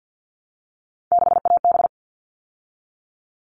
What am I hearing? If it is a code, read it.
Morse code, heard as a keyed tone, about 48 wpm, 700 Hz. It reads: 6AB
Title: All Cut Numbers - 3-Digit 50wpm